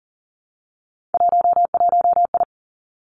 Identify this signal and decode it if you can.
Morse code, heard as a keyed tone, about 41 wpm, 700 Hz. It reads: 11I
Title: Three Challenging Letter-Number Combinations 40wpm